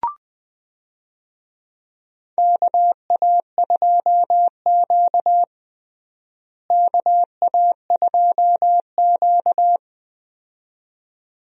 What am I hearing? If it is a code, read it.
Morse code, heard as a keyed tone, about 20 wpm, 700 Hz. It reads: KA2Q KA2Q
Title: USA Call Signs - 4 Characters - Repeated 2x - 20wpm